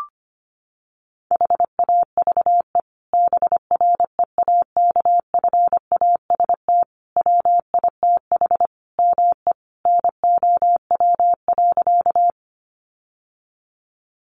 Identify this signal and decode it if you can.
Morse code, heard as a keyed tone, about 25 wpm, 700 Hz. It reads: HAVE BREAKFAST WITH ME NOW.
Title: Sentences from Top 1500 Words 25wpm